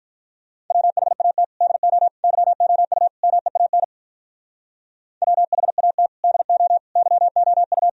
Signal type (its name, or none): Morse code